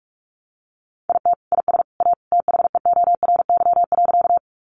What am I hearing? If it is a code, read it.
Morse code, heard as a keyed tone, about 44 wpm, 700 Hz. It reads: IT IS A THEORY.
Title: Sentences from Top 600 Words 45wpm